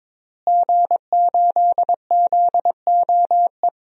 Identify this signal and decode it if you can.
Morse code, heard as a keyed tone, about 22 wpm, 700 Hz. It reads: G8ZOE
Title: DXCC-100 Call Signs - 5 Characters - Repeated 2x - 22wpm